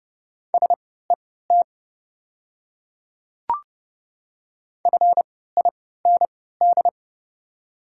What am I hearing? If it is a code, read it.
Morse code, heard as a keyed tone, about 30 wpm, spaced as for 17 wpm, 700 Hz. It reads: SET FIND
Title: 3x Character Spacing - Top 200 Words - 30wpm